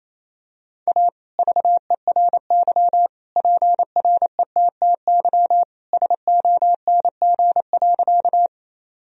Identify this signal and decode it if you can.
Morse code, heard as a keyed tone, about 28 wpm, 700 Hz. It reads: A VERY PRETTY SONG.
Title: Sentences from Top 1000 Words - Review Part 2 - 28wpm